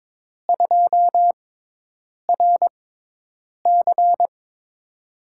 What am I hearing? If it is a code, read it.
Morse code, heard as a keyed tone, about 22 wpm, spaced as for 8 wpm, 700 Hz. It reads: 2RC
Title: USA Call Signs - 5 Characters - 6x Character Spacing - 22wpm